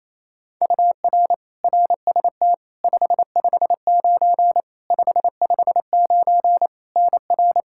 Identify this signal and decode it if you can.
Morse code, heard as a keyed tone, about 27 wpm, 700 Hz. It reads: UR RST 559 559 NR
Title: Practice QSO - Exchange 1 - Beginning - 28wpm